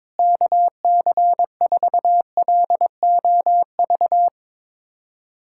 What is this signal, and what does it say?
Morse code, heard as a keyed tone, about 23 wpm, 700 Hz. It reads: KC4LOV
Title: USA Callsign DE Callsign 22wpm